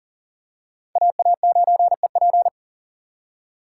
Morse code, heard as a keyed tone, about 40 wpm, 700 Hz. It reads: AA9EP
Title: USA Call Signs - 6x Character Spacing - 40wpm